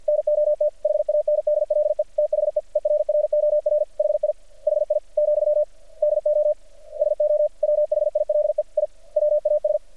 Morse code, heard as a keyed tone, about 50 wpm, 600 Hz. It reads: NOT HANDLE THE ERROR HI HI , DO SO WHILE I WAS
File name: archived / CRSnet